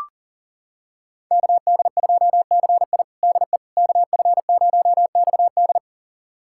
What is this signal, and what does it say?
Morse code, heard as a keyed tone, about 40 wpm, 700 Hz. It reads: KD2CI DE KF0XD
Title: USA Callsign DE Callsign - 5 Characters - 40wpm